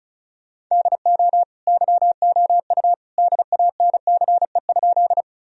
Morse code, heard as a keyed tone, about 36 wpm, 700 Hz. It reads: DO YOU DANCE?
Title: Sentences from Top 1000 Words - Review Part 4 - 35wpm